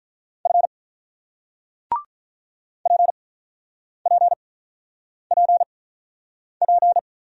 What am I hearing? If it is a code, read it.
Morse code, heard as a keyed tone, about 47 wpm, spaced as for 9 wpm, 700 Hz. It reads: F PPPP